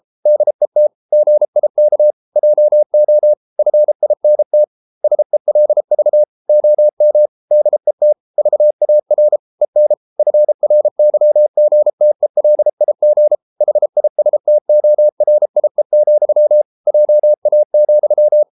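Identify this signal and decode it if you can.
Morse code, heard as a keyed tone, about 33 wpm, 600 Hz. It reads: DET GIK JO FINT SELV OM DET VAR EN FRYGTELIG HISTORIE, JA,
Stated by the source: Morse code at 33 wpm, QRQ.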